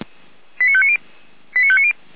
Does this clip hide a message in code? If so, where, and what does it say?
Yes: five-tone selective call (ZVEI) from 0.6 to 1.9 s, heard as quick runs of five tones, tones of about 70 ms. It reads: 89590 89590